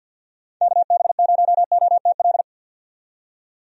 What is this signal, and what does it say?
Morse code, heard as a keyed tone, about 51 wpm, 700 Hz. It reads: KB0OTL
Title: USA Call Signs - 6 Characters - 50wpm